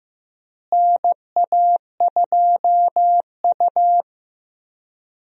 Morse code, heard as a keyed tone, about 15 wpm, 700 Hz. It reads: NA2U
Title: USA Call Signs - 4 Characters - Repeated 2x - 15wpm